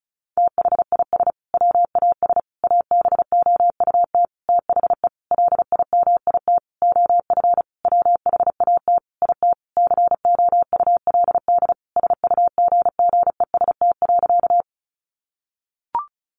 Morse code, heard as a keyed tone, about 34 wpm, 700 Hz. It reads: THIS WAS ABOUT THE LIMIT OF WHAT IT COULD SUGGEST.
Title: Sentences from Top 1500 Words 35wpm